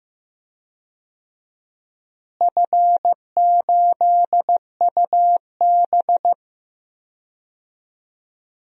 Morse code, heard as a keyed tone, about 15 wpm, 700 Hz. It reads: F8UB
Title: DXCC-100 Call Signs - 4 Characters - Repeated 2x - 15wpm